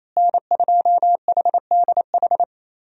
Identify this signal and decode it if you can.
Morse code, heard as a keyed tone, about 28 wpm, 700 Hz. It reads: N2HDH
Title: USA Call Signs - 5 Characters - Repeated 2x - 28wpm